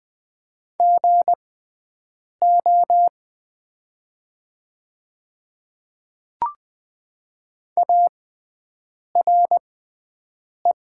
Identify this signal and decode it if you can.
Morse code, heard as a keyed tone, about 20 wpm, spaced as for 7 wpm, 700 Hz. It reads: GO ARE
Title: 6x Character Spacing - Top 100 Words - 20wpm